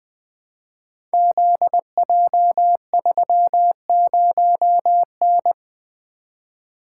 Morse code, heard as a keyed tone, about 20 wpm, 700 Hz. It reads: ZJ30N
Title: Code Groups - 5 Letter:Numbers - 20wpm